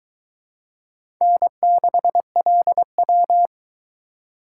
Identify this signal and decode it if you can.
Morse code, heard as a keyed tone, about 22 wpm, 700 Hz. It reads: N6LW